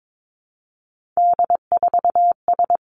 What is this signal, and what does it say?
Morse code, heard as a keyed tone, about 22 wpm, 700 Hz. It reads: D4S
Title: Three Challenging Letter-Number Combinations 22wpm